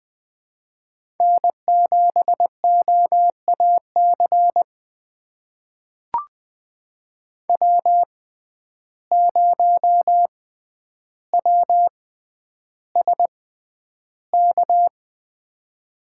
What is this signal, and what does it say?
Morse code, heard as a keyed tone, about 20 wpm, spaced as for 10 wpm, 700 Hz. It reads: N7OAC W0WSK
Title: USA Call Signs - 6x Character Spacing - 20wpm